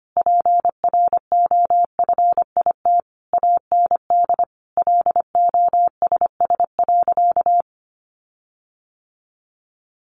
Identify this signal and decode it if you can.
Morse code, heard as a keyed tone, about 25 wpm, 700 Hz. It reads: PROFIT AND LOSS.
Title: Sentences from Top 1400 Words 25wpm